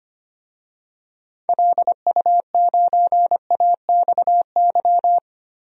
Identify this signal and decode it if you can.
Morse code, heard as a keyed tone, about 25 wpm, 700 Hz. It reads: LU9AXY
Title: DXCC-100 Call Signs - 6 Characters - 25wpm